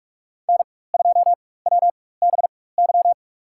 Morse code, heard as a keyed tone, about 46 wpm, spaced as for 21 wpm, 700 Hz. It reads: N2WBY
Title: USA Call Signs - 5 Characters - 4x Character Spacing - 45wpm